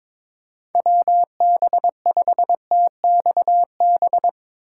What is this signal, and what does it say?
Morse code, heard as a keyed tone, about 24 wpm, 700 Hz. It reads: WB5TXB
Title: Speed Racing - USA 2x3 Call Signs 22wpm